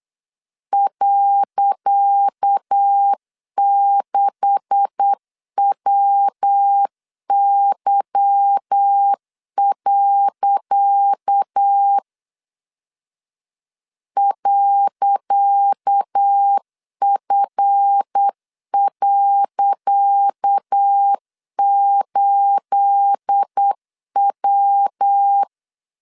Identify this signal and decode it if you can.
Morse code, heard as a keyed tone, about 8 wpm, 800 Hz. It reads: .6WY. .F.8W
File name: LEZIONI / 10 SIGNS / 10C DOT 8wpm 3649